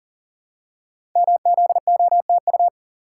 Morse code, heard as a keyed tone, about 38 wpm, 700 Hz. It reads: MZOTU
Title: Code Groups - 5 Letters - 40wpm